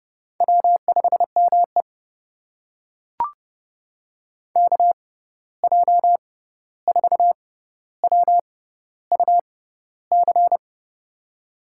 Morse code, heard as a keyed tone, about 31 wpm, spaced as for 14 wpm, 700 Hz. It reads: W5ME KJ4WUC
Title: USA Call Signs - 6x Character Spacing - 30wpm